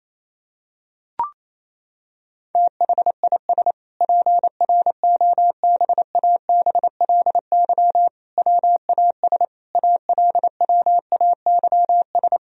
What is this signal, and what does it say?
Morse code, heard as a keyed tone, about 28 wpm, 700 Hz. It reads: THIS PROBABLY WAS ALWAYS
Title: Sentences from Top 1000 Words - Review Part 4 - 28wpm